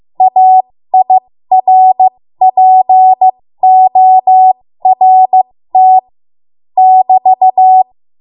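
Morse code, heard as a keyed tone, about 15 wpm, spaced as for 13 wpm, 750 Hz. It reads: AIRPORT <BT>
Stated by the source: 13 wpm Morse code practice archive transmission.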